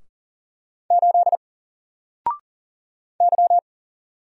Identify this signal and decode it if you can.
Morse code, heard as a keyed tone, about 40 wpm, 700 Hz. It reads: 8 Y
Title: Single Letter-Number 40wpm